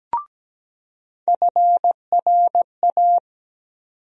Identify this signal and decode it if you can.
Morse code, heard as a keyed tone, about 17 wpm, 700 Hz. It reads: FRA
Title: USA Call Sign Postfixes - 3 Letters 17wpm